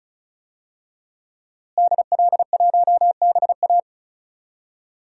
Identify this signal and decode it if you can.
Morse code, heard as a keyed tone, about 34 wpm, 700 Hz. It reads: DL1BA
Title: DXCC-100 Call Signs - 5 Characters - Repeated 2x - 35wpm